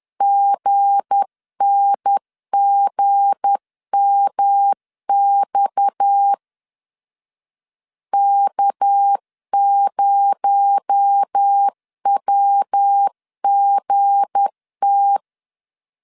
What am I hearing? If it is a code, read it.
Morse code, heard as a keyed tone, about 10 wpm, 800 Hz. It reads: GNGMX K0WGT
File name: LEZIONI / 7 GF9Z / 7 G 10wpm 2501